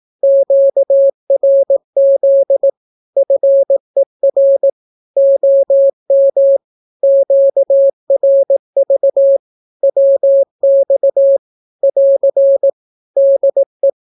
Morse code, heard as a keyed tone, about 18 wpm, 550 Hz. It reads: QRZ FER OM QRV WX <AR> DE
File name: naukapodstaw / skroty / skroty